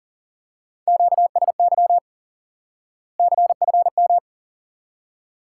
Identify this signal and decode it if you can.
Morse code, heard as a keyed tone, about 40 wpm, 700 Hz. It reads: QSY CFM
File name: IQE Course 3 - Lesson 89 - YAGI - Two Elements - 40wpm